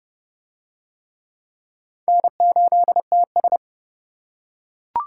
Morse code, heard as a keyed tone, about 28 wpm, 700 Hz. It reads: N8TS